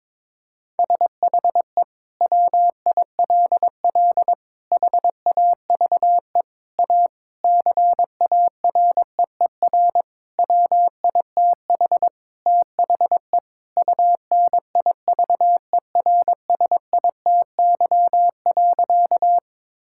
Morse code, heard as a keyed tone, about 22 wpm, 700 Hz. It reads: SHE WILL HAVE A CAREER WITH THE UNIVERSITY.
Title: Sentences from Top 1000 Words - Review Part 6 - 22wpm